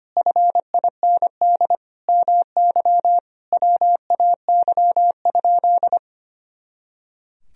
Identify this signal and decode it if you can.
Morse code, heard as a keyed tone, about 25 wpm, 700 Hz. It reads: FIND MY WAY?
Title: Sentences from Top 200 Words and Abbreviations 25wpm